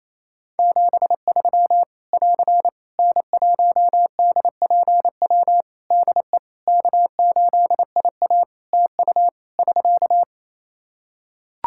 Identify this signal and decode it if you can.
Morse code, heard as a keyed tone, about 28 wpm, 700 Hz. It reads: 73 <AR> N1DPW DE K8IA TU <SK>
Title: K1USN SST QSO Practice 28wpm